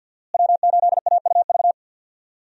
Morse code, heard as a keyed tone, about 50 wpm, 700 Hz. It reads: W8AUV
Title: USA Call Signs - 5 Characters - Repeated 2x - 50wpm